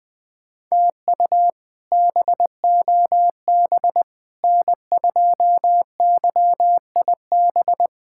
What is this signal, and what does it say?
Morse code, heard as a keyed tone, about 20 wpm, 700 Hz. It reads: TU BOB N2YIB
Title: K1USN SST QSO Practice - 20wpm